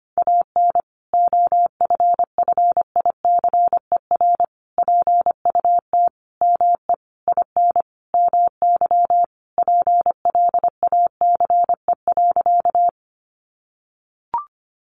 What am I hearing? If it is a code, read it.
Morse code, heard as a keyed tone, about 25 wpm, 700 Hz. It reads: AN OFFICER PUT ME IN MY PLACE.
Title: Sentences from Top 1200 Words 25wpm